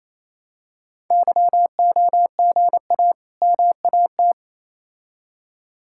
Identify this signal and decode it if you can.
Morse code, heard as a keyed tone, about 27 wpm, 700 Hz. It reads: YOGA MAT
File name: Easy Words - Speed-Racing with Hints - 22wpm